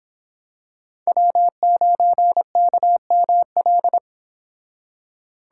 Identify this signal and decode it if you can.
Morse code, heard as a keyed tone, about 27 wpm, 700 Hz. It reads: W9KML
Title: Speed Racing - USA 1x3 Call Signs 22wpm